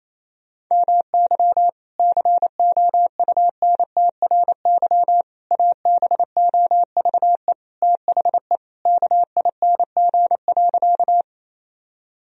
Morse code, heard as a keyed tone, about 28 wpm, 700 Hz. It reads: MY COUNTRY ABOVE THE KING.
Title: Sentences from Top 1000 Words - Review Part 2 - 28wpm